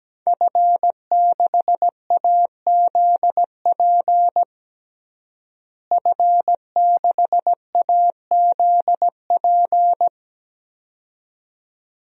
Morse code, heard as a keyed tone, about 17 wpm, 700 Hz. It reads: F6AZP F6AZP
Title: DXCC-100 Call Signs - 5 Characters - Repeated 2x - 17wpm